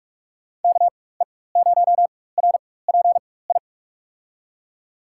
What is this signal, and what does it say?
Morse code, heard as a keyed tone, about 44 wpm, spaced as for 21 wpm, 700 Hz. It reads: KE0RPI
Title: USA Call Signs - 6 Characters - 4x Character Spacing - 45wpm